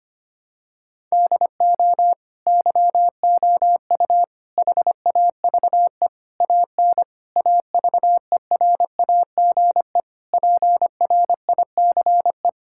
Morse code, heard as a keyed tone, about 25 wpm, 700 Hz. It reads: DO YOU HAVE AN AVERAGE PRICE
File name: practice / 25wpm / Sentences with 9-Word Phrases - 25wpm